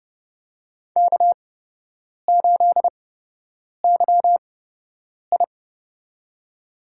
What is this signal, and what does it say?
Morse code, heard as a keyed tone, about 30 wpm, spaced as for 8 wpm, 700 Hz. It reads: K8YI
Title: USA Call Signs - 4 Characters - 8x Character Spacing - 30wpm